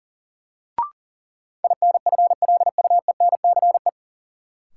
Morse code, heard as a keyed tone, about 40 wpm, 700 Hz. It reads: INFLUENCE
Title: Sentences from Top 1100 Words - New Word Drill - 40wpm